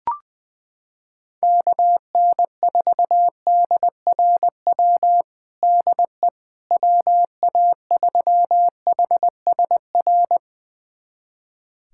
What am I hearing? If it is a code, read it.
Morse code, heard as a keyed tone, about 21 wpm, 700 Hz. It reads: KN4DRW DE WA3HSR
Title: USA Callsign DE Callsign 20wpm